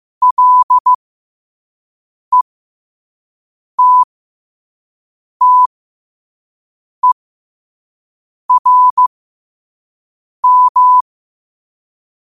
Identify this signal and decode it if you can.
Morse code, heard as a keyed tone, about 15 wpm, spaced as for 5 wpm, 1000 Hz. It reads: LETTERM
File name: 5wpm